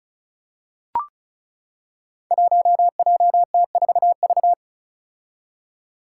Morse code, heard as a keyed tone, about 36 wpm, 700 Hz. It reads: 1JT4V
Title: Five Challenging Letter-Number Combinations 35wpm